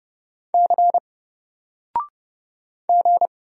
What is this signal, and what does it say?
Morse code, heard as a keyed tone, about 31 wpm, 700 Hz. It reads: C G
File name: Single-Letter-Number-30wpm